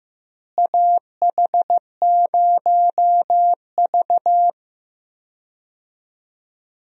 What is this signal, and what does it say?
Morse code, heard as a keyed tone, about 15 wpm, 700 Hz. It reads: AH0V
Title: USA Call Signs - 4 Characters - Repeated 3x - 15wpm